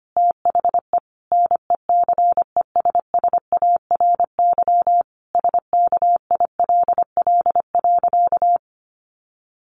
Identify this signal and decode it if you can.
Morse code, heard as a keyed tone, about 25 wpm, 700 Hz. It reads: THE NECESSARY SKILL.
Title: Sentences from Top 1200 Words 25wpm